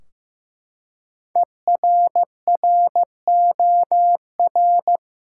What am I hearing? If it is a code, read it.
Morse code, heard as a keyed tone, about 15 wpm, 700 Hz. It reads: ERROR
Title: Five Letter Words - 15wpm